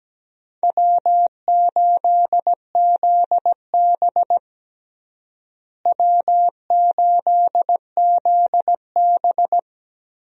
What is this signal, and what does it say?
Morse code, heard as a keyed tone, about 17 wpm, 700 Hz. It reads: W8ZB W8ZB
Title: USA Call Signs - 4 Characters - Repeated 3x - 17wpm